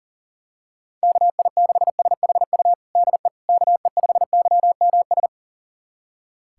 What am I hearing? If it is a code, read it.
Morse code, heard as a keyed tone, about 37 wpm, 700 Hz. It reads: KI6SHU DE KE5YMS
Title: USA Callsign DE Callsign - 6 Characters - 40wpm